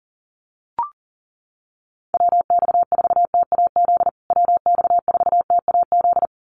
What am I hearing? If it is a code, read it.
Morse code, heard as a keyed tone, about 37 wpm, 700 Hz. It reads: WX4TAZ WX4TAZ
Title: USA Call Signs Repeated 2x 40wpm